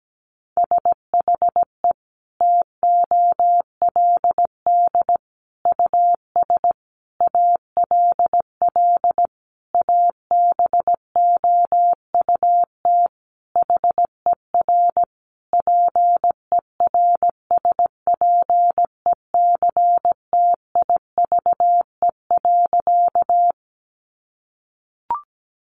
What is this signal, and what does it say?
Morse code, heard as a keyed tone, about 17 wpm, 700 Hz. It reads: SHE TOLD US ALL ABOUT HER PERSPECTIVE.
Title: Sentences from Top 1200 Words 17wpm